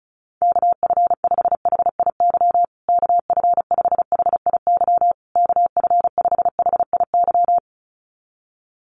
Morse code, heard as a keyed tone, about 34 wpm, 700 Hz. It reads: KF5HIY KF5HIY KF5HIY
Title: USA Call Signs Repeated 3x 35wpm